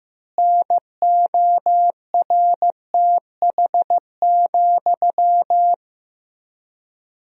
Morse code, heard as a keyed tone, about 15 wpm, 700 Hz. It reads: NORTH,
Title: 3x Word-Spacing - Sentences from Top 400 Words - 15wpm